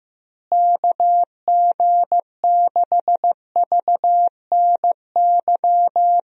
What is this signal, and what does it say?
Morse code, heard as a keyed tone, about 16 wpm, 700 Hz. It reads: KG6VNY
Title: USA Call Signs - 6 Characters - Repeated 3x - 15wpm